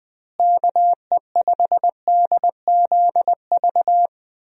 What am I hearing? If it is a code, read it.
Morse code, heard as a keyed tone, about 20 wpm, 700 Hz. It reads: KE5DZV
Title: USA Call Signs - 6 Characters - Repeated 2x - 20wpm